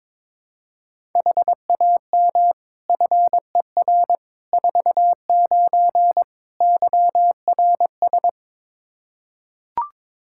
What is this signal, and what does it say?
Morse code, heard as a keyed tone, about 22 wpm, 700 Hz. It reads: HAM FER 49 YRS
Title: Practice QSO - Exchange 3 - More Info - 22wpm